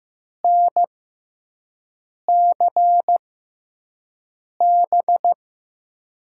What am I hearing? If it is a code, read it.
Morse code, heard as a keyed tone, about 15 wpm, spaced as for 5 wpm, 700 Hz. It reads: NCB
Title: USA Call Signs - 5 Characters - 6x Character Spacing - 15wpm